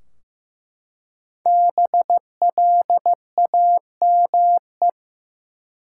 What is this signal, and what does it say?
Morse code, heard as a keyed tone, about 15 wpm, 700 Hz. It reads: BLAME
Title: Five Letter Words - 15wpm